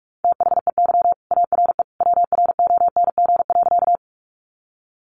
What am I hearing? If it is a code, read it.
Morse code, heard as a keyed tone, about 44 wpm, 700 Hz. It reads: THEY ARE WRONG.
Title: Sentences from Top 900 Words 45wpm